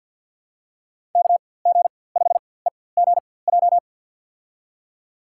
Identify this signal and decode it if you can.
Morse code, heard as a keyed tone, about 47 wpm, spaced as for 23 wpm, 700 Hz. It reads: KG5EGJ
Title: USA Call Signs - 6 Characters - 4x Character Spacing - 50wpm